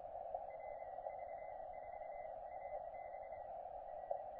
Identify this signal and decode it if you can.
Morse code, heard as a keyed tone, about 25 wpm, spaced as for 21 wpm, 2050 Hz, with radio noise. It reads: RV4AB
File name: OX3LX, NA-243 30CW